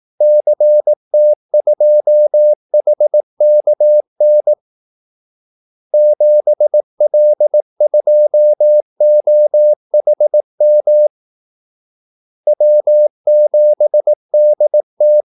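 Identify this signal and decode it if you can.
Morse code, heard as a keyed tone, about 18 wpm, 600 Hz. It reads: CT2HKN 7L2OHM W7DT
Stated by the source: Morse call-sign practice at 18 wpm.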